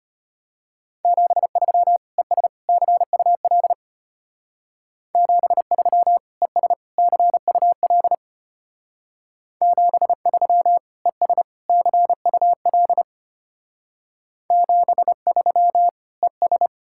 Morse code, heard as a keyed tone, about 29 wpm, 700 Hz. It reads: 73 ES CUL 73 ES CUL 73 ES CUL 73 ES
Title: Speed-Racing Common QSO Phrases - 25wpm